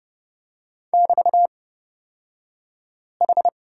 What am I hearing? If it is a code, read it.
Morse code, heard as a keyed tone, about 30 wpm, spaced as for 5 wpm, 690 Hz. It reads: <BT>H
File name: Lesson 36d - BT - Two Characters 30wpm